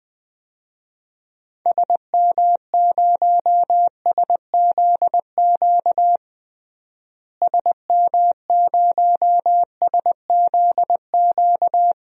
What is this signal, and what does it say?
Morse code, heard as a keyed tone, about 20 wpm, 700 Hz. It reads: SM0SZQ SM0SZQ
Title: DXCC-100 Call Signs - 6 Characters - Repeated 3x - 20wpm